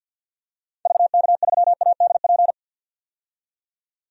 Morse code, heard as a keyed tone, about 49 wpm, 700 Hz. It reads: VK3ADP
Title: DXCC-100 Call Signs - 6 Characters - 50wpm